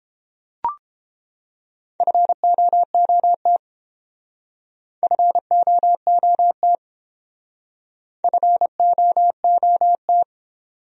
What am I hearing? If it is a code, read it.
Morse code, heard as a keyed tone, about 28 wpm, 700 Hz. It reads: FOOT FOOT FOOT